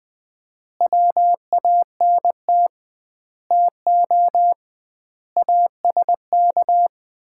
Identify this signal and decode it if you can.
Morse code, heard as a keyed tone, about 20 wpm, 700 Hz. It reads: WANT TO ASK
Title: 2x Word-Spacing - Sets of 5 Words - 20wpm